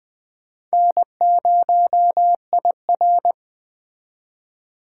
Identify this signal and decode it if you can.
Morse code, heard as a keyed tone, about 20 wpm, 700 Hz. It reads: N0IR